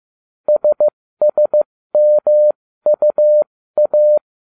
Morse code, heard as a keyed tone, about 15 wpm, spaced as for 13 wpm, 600 Hz. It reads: SSMUA